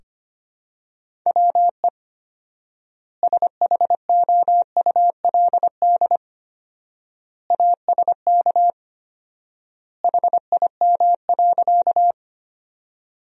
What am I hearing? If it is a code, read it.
Morse code, heard as a keyed tone, about 25 wpm, 700 Hz. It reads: WE SHOULD ASK HIM.
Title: 4x Word-Spacing - Sentences from Top 300 Words - 25wpm